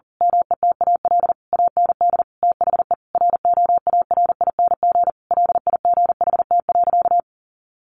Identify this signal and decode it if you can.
Morse code, heard as a keyed tone, about 40 wpm, 700 Hz. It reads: METAL AND THE ROARING LIGHT.
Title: A Sound Of Thunder 40wpm